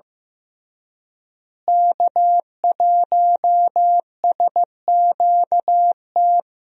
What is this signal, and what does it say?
Morse code, heard as a keyed tone, about 15 wpm, 700 Hz. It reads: K1SQT
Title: USA Call Signs - 5 Characters - Repeated 3x - 15wpm